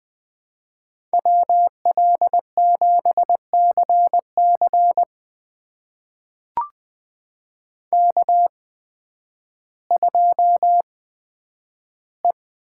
Morse code, heard as a keyed tone, about 20 wpm, spaced as for 11 wpm, 700 Hz. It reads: WL7CC K2E
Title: USA Call Signs - 8x Character Spacing - 20wpm